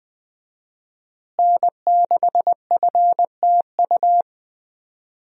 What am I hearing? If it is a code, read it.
Morse code, heard as a keyed tone, about 20 wpm, 700 Hz. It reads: N6FTU
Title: USA Call Signs - 5 Characters - Repeated 3x - 20wpm